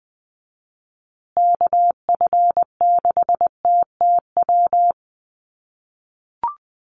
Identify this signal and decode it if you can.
Morse code, heard as a keyed tone, about 20 wpm, 700 Hz. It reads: KF6TTW
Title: USA Call Signs Repeated 3x 20wpm